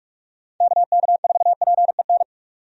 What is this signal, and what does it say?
Morse code, heard as a keyed tone, about 46 wpm, 700 Hz. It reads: KK4PEN
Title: USA Call Signs - 6 Characters - 45wpm